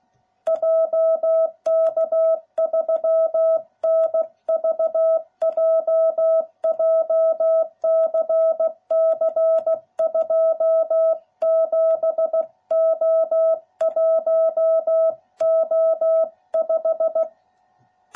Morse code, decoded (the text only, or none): JK3NVJ JCC 27O1O5